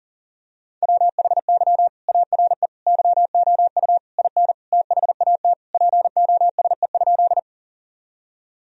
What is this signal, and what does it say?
Morse code, heard as a keyed tone, about 40 wpm, 700 Hz. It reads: WHY ARE YOU IN THAT POSE?
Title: Sentences from Top 1000 Words - Review Part 2 - 40wpm